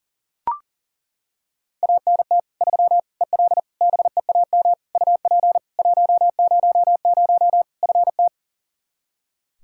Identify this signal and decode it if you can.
Morse code, heard as a keyed tone, about 40 wpm, 700 Hz. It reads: ANT 3 EL BEAM UP 100 FT
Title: Practice QSO - Exchange 2 - Info - 40wpm